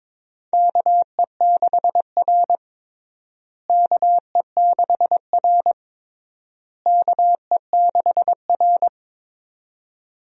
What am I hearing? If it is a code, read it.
Morse code, heard as a keyed tone, about 22 wpm, 700 Hz. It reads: KE6R KE6R KE6R
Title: USA Call Signs - 4 Characters - Repeated 3x - 22wpm